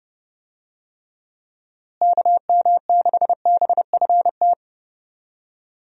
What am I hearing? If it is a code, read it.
Morse code, heard as a keyed tone, about 30 wpm, 700 Hz. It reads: KM6BFT
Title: USA Call Signs - 6 Characters - Repeated 3x - 30wpm